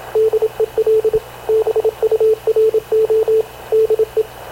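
Morse code, heard as a keyed tone, about 27 wpm, 430 Hz, with radio noise. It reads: DEL BURO DE
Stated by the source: Havana Radio / CLA sending Spanish press (PX) in Morse, recorded by Radio Officer Paul Dumesnil onboard CNAV QUEST/CZDO.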